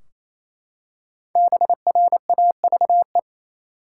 Morse code, heard as a keyed tone, about 28 wpm, 700 Hz. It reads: BRAVE
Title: Five Letter Words - 28wpm